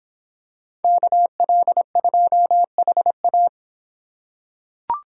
Morse code, heard as a keyed tone, about 25 wpm, 700 Hz. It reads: KL2HA